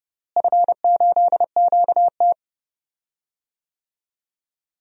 Morse code, heard as a keyed tone, about 30 wpm, 700 Hz. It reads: F8QT
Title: DXCC-100 Call Signs - 4 Characters - Repeated 3x - 30wpm